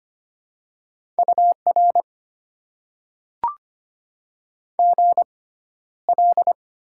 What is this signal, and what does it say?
Morse code, heard as a keyed tone, about 25 wpm, spaced as for 13 wpm, 690 Hz. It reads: UR GL
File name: Top 90 QSO Elements - 6x Character Spacing - 25wpm